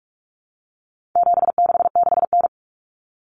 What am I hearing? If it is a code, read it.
Morse code, heard as a keyed tone, about 44 wpm, 700 Hz. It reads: 766N
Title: Common Cut Numbers - 4-Digit 45wpm